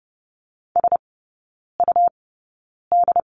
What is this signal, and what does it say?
Morse code, heard as a keyed tone, about 30 wpm, spaced as for 9 wpm, 690 Hz. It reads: SUD
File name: Lesson 17e - Letter u - Three Characters 30wpm